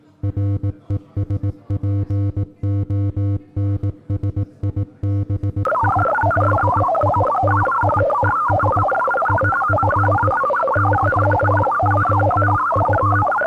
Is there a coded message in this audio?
Yes: Morse code from 0.2 to 13.2 s, heard as a keyed tone, about 18 wpm, 310 Hz. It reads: RESPONSIBILITIES IF YOU